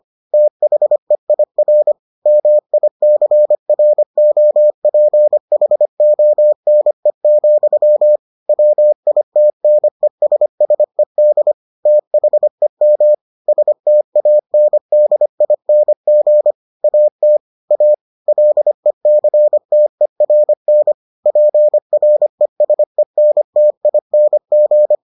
Morse code, heard as a keyed tone, about 25 wpm, 600 Hz. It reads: THEIR MICROPHONE, WITNESSED THEM STANDING AT A LECTERN PRESENTING